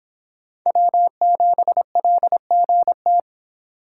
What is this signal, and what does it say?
Morse code, heard as a keyed tone, about 26 wpm, 700 Hz. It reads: W7LGT